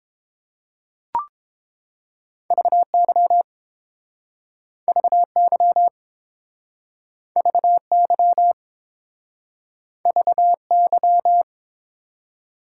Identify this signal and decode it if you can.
Morse code, heard as a keyed tone, about 30 wpm, 700 Hz. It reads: VY VY VY VY